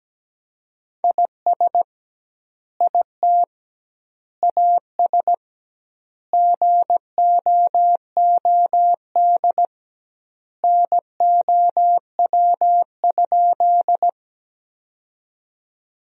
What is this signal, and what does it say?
Morse code, heard as a keyed tone, about 17 wpm, 700 Hz. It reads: IS IT AS GOOD NOW?
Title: 2x Word-Spacing - Sentences from Top 200 Words - 17wpm